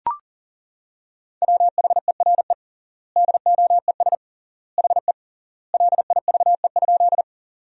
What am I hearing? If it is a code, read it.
Morse code, heard as a keyed tone, about 40 wpm, 700 Hz. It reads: WHERE DOES HE LIVE?
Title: 3x Word-Spacing - Sentences from Top 200 Words - 40wpm